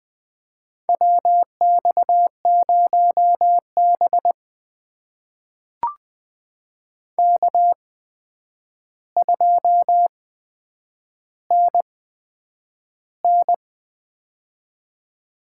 Morse code, heard as a keyed tone, about 20 wpm, spaced as for 9 wpm, 700 Hz. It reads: WX0B K2NN
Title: USA Call Signs - 4 Characters - 8x Character Spacing - 20wpm